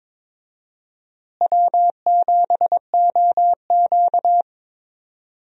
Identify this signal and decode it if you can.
Morse code, heard as a keyed tone, about 21 wpm, 700 Hz. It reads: W7OQ
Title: USA Call Signs - 4 Characters - Repeated 3x - 22wpm